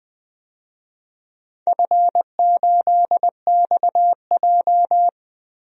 Morse code, heard as a keyed tone, about 20 wpm, 700 Hz. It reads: F8XJ
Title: DXCC-100 Call Signs - 4 Characters - 20wpm